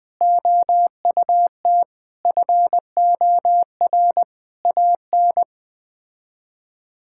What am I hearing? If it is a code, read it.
Morse code, heard as a keyed tone, about 21 wpm, 700 Hz. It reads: OUT FOR AN
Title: Sets of 6 Words - 20wpm